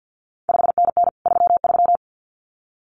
Morse code, heard as a keyed tone, about 50 wpm, 700 Hz. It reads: 5NN 33
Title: CQ WW DX Contest Exchange - 50wpm